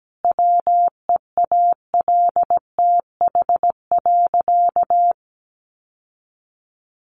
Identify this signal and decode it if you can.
Morse code, heard as a keyed tone, about 17 wpm, 700 Hz. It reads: WEALTH.
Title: Sentences from Top 1200 Words 17wpm